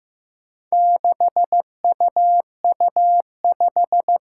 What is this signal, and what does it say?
Morse code, heard as a keyed tone, about 15 wpm, 700 Hz. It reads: 6UU5
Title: Four Challenging Letter-Number Combinations 15wpm